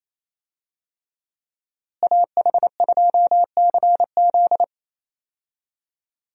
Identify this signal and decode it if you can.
Morse code, heard as a keyed tone, about 28 wpm, 700 Hz. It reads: AH2CZ
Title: USA Call Signs - 5 Characters - Repeated 3x - 28wpm